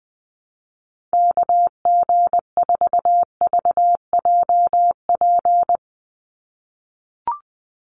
Morse code, heard as a keyed tone, about 20 wpm, 700 Hz. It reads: KG4VJP
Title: USA Call Signs 20wpm